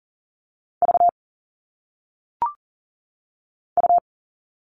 Morse code, heard as a keyed tone, about 39 wpm, 700 Hz. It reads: V U
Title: All Cut Numbers - 1-Digit 40wpm